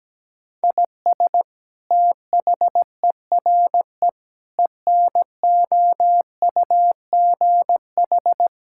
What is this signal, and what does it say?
Morse code, heard as a keyed tone, about 17 wpm, 700 Hz. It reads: IS THERE ENOUGH
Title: Sentences from Top 1000 Words - Review Part 5 - 17wpm